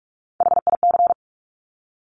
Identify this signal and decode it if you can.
Morse code, heard as a keyed tone, about 46 wpm, 690 Hz. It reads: HIC
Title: Top Three Letter Combinations in Words 45wpm